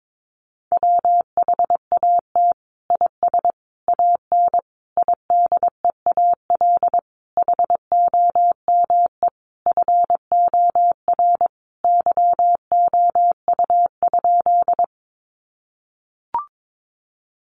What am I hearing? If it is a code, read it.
Morse code, heard as a keyed tone, about 22 wpm, 700 Hz. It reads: WHAT IS AN IDEAL HOME FOR YOU?
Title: Sentences from Top 1400 Words 22wpm